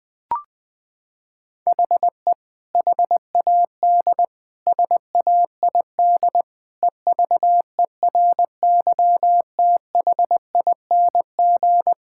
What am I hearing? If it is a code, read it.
Morse code, heard as a keyed tone, about 21 wpm, 700 Hz. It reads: HE HAD SAID EVERYTHING
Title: Sentences from Top 1600 Words 20wpm